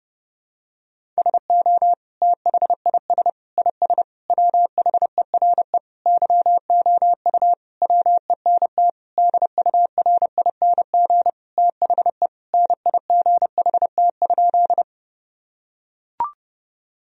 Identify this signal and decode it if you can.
Morse code, heard as a keyed tone, about 30 wpm, 700 Hz. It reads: SO THIS IS WHERE YOU WENT DURING THE NIGHT?
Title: Sentences from Top 1000 Words - Review Part 2 - 30wpm